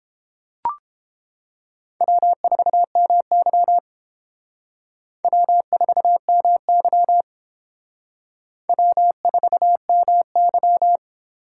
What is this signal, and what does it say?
Morse code, heard as a keyed tone, about 31 wpm, 700 Hz. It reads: W4MY W4MY W4MY